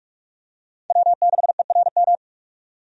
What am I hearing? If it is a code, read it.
Morse code, heard as a keyed tone, about 43 wpm, 700 Hz. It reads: W6ERM
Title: USA Call Signs - 5 Characters - 45wpm